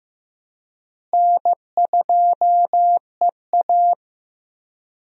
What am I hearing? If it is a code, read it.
Morse code, heard as a keyed tone, about 15 wpm, 700 Hz. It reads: N2EA
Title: USA Call Signs - 4 Characters - Repeated 3x - 15wpm